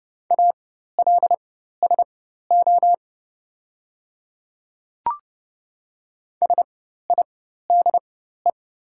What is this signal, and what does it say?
Morse code, heard as a keyed tone, about 30 wpm, spaced as for 14 wpm, 690 Hz. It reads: ALSO SIDE
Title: 4x Character Spacing - Top 200 Words - 30wpm